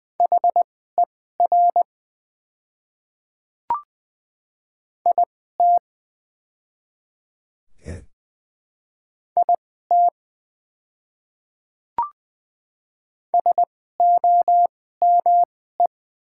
Morse code, heard as a keyed tone, about 20 wpm, spaced as for 14 wpm, 700 Hz. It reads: HER IT IT SOME
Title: 2x Character Spacing - Top 100 Words - 20wpm